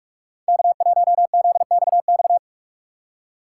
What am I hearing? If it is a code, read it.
Morse code, heard as a keyed tone, about 46 wpm, 700 Hz. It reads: K1ZXX
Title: USA Call Signs - 5 Characters - 4x Character Spacing - 45wpm